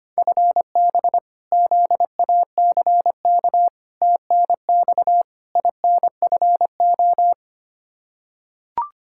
Morse code, heard as a keyed tone, about 25 wpm, 700 Hz. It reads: FB ZACK TNX INFO
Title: Practice QSO - Exchange 3 - More Info - 25wpm